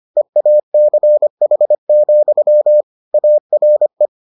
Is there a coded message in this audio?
Yes: Morse code from 0.2 to 4.1 s, heard as a keyed tone, about 25 wpm, 600 Hz. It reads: EACH, ARE